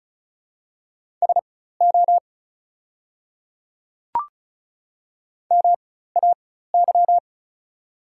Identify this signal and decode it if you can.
Morse code, heard as a keyed tone, about 36 wpm, spaced as for 16 wpm, 700 Hz. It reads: SO MAY
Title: 4x Character Spacing - Top 100 Words - 35wpm